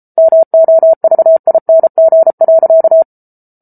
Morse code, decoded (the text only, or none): MOVING.